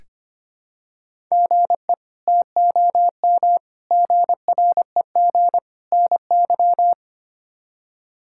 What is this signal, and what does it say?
Morse code, heard as a keyed tone, about 25 wpm, 700 Hz. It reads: GE TOM GREG NY
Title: K1USN SST QSO Practice - 25wpm